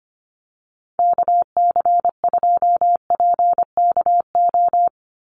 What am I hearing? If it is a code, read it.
Morse code, heard as a keyed tone, about 25 wpm, 700 Hz. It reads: KC2PKO
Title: USA Call Signs 25wpm